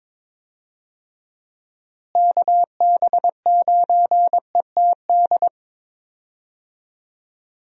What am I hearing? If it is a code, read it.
Morse code, heard as a keyed tone, about 22 wpm, 700 Hz. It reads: KB9ETD
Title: USA Call Signs - 6 Characters - Repeated 3x - 22wpm